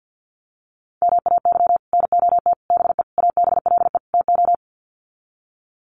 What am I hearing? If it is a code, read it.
Morse code, heard as a keyed tone, about 50 wpm, 700 Hz. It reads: MAY NOT BE ABLE TO
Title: Sets of 5 Words 50wpm